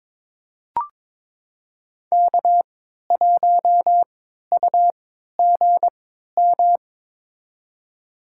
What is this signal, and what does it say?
Morse code, heard as a keyed tone, about 22 wpm, spaced as for 12 wpm, 700 Hz. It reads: K1UGM
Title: USA Call Signs - 5 Characters - 3x Character Spacing - 22wpm